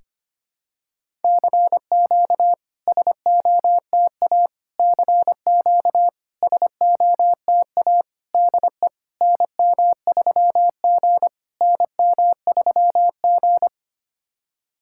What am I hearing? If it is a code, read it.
Morse code, heard as a keyed tone, about 25 wpm, 690 Hz. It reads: CQ SOTA CQ SOTA DE NM3G NM3G
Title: SOTA QSO Practice 25wpm